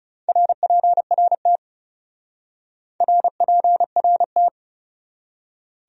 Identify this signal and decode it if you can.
Morse code, heard as a keyed tone, about 37 wpm, 700 Hz. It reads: RPRT RPRT
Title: Speed-Racing Abbreviations - 30wpm